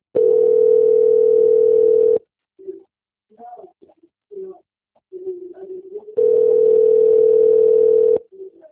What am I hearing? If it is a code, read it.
Ringback tone (North America), 440 + 480 Hz.
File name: out- - - - - .66789